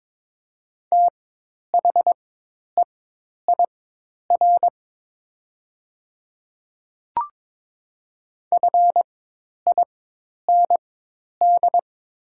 Morse code, heard as a keyed tone, about 22 wpm, spaced as for 10 wpm, 700 Hz. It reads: THEIR FIND
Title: 4x Character Spacing - Top 100 Words - 22wpm